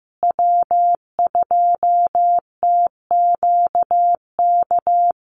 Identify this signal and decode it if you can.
Morse code, heard as a keyed tone, about 15 wpm, 700 Hz. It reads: W2TQK
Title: USA Call Signs Repeated 3x 15wpm